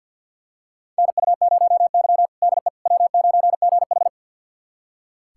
Morse code, heard as a keyed tone, about 45 wpm, 700 Hz. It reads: NU0Y DE W9GH
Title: USA Callsign DE Callsign - 4 Characters - 50wpm